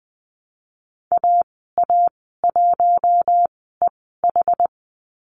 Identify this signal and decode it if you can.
Morse code, heard as a keyed tone, about 20 wpm, spaced as for 15 wpm, 700 Hz. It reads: AA1EH
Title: 069 Lesson 12g - Letter d - Call Signs 20wpm